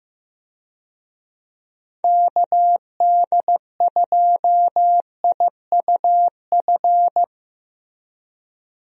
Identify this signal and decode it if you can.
Morse code, heard as a keyed tone, about 15 wpm, 700 Hz. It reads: KD2IUF
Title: USA Call Signs - 6 Characters - Repeated 2x - 15wpm